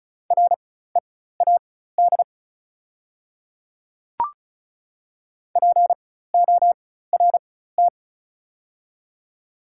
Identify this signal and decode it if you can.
Morse code, heard as a keyed tone, about 36 wpm, spaced as for 16 wpm, 700 Hz. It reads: READ PORT
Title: 4x Character Spacing - Top 200 Words - 35wpm